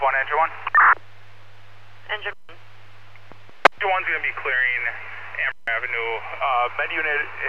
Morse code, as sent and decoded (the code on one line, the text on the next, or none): none